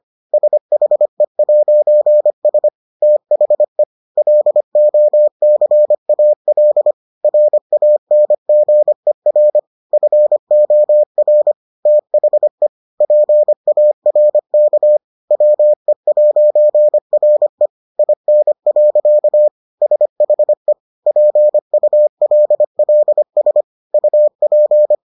Morse code, heard as a keyed tone, about 25 wpm, 600 Hz. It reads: SHE'S THE LOCAL RANGER FOR THE PARK WE'RE IN. SHE PULLS UP